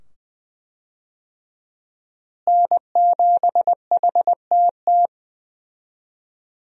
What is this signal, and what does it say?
Morse code, heard as a keyed tone, about 20 wpm, 700 Hz. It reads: N7HTT
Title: USA Call Signs - 5 Characters - Repeated 2x - 20wpm